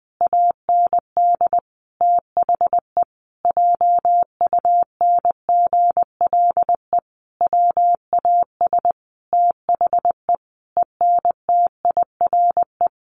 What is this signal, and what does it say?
Morse code, heard as a keyed tone, about 20 wpm, 700 Hz. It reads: AND THE JUNGLE WAS THE ENTIRE
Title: A Sound Of Thunder 20wpm